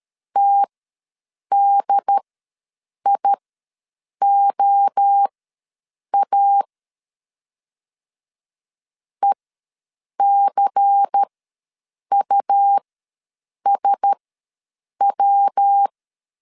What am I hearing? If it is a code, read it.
Morse code, heard as a keyed tone, about 13 wpm, spaced as for 7 wpm, 800 Hz. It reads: TDIOA ECUSW
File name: LEZIONI / 5 LBVC / 5 LBVC ws 12wpm 3159